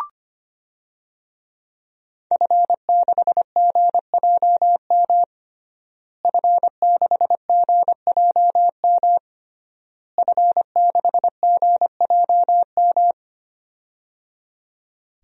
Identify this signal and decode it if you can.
Morse code, heard as a keyed tone, about 25 wpm, 700 Hz. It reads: F6GJM F6GJM F6GJM
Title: DXCC-100 Call Signs - 5 Characters - Repeated 3x - 25wpm